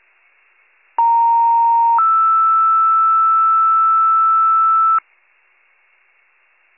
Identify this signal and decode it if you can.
Two-tone page, heard as a one-second tone, then a three-second tone, about 930 then 1320 Hz. It is heard from 1.0 to 5.0 s.